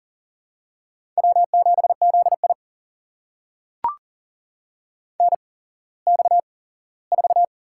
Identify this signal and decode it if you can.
Morse code, heard as a keyed tone, about 40 wpm, spaced as for 19 wpm, 700 Hz. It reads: W7ZI NX4
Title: USA Call Signs - 4 Characters - 8x Character Spacing - 40wpm